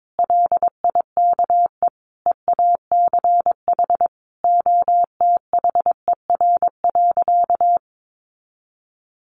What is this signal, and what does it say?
Morse code, heard as a keyed tone, about 22 wpm, 700 Hz. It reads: LIKE EACH OTHER.
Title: Easy Phrases 22wpm